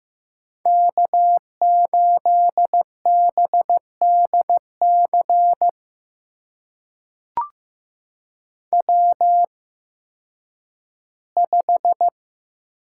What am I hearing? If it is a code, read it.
Morse code, heard as a keyed tone, about 15 wpm, spaced as for 10 wpm, 700 Hz. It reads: K8BDC W5
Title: USA Call Signs - 8x Character Spacing - 15wpm